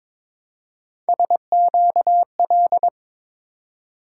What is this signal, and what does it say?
Morse code, heard as a keyed tone, about 23 wpm, 700 Hz. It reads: SQL
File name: practice / 22wpm / Three Letter-Number Combinations - 22wpm